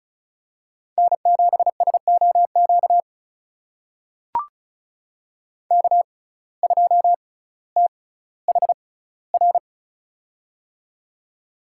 Morse code, heard as a keyed tone, about 36 wpm, spaced as for 18 wpm, 700 Hz. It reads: N7SOQ K2THR
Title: USA Call Signs - 5 Characters - 6x Character Spacing - 35wpm